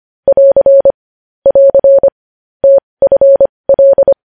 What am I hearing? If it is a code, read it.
Morse code, heard as a keyed tone, about 25 wpm, spaced as for 20 wpm, 550 Hz. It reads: <AR> <AR> TFL
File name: podcast / News headlines-1x-25x20